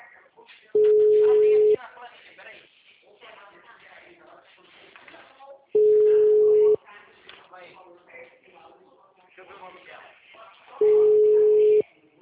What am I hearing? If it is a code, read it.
Ringback tone (Europe, 425 Hz), with speech.
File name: out- -2008- - - .57021